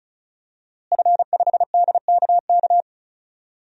Morse code, heard as a keyed tone, about 34 wpm, 700 Hz. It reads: F5DKK
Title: DXCC-100 Call Signs - 5 Characters - Repeated 3x - 35wpm